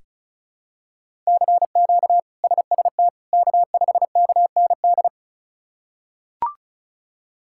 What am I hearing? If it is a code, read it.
Morse code, heard as a keyed tone, about 36 wpm, 700 Hz. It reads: CQ SST K5KND
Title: K1USN SST QSO Practice - 35wpm